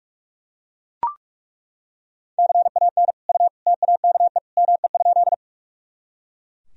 Morse code, heard as a keyed tone, about 42 wpm, 700 Hz. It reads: CAN U TAKE ME?
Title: Sentences from Top 200 Words and Abbreviations 45wpm